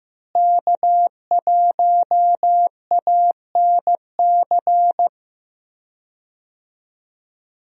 Morse code, heard as a keyed tone, about 16 wpm, 700 Hz. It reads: K1ANC
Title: USA Call Signs - 5 Characters - Repeated 2x - 15wpm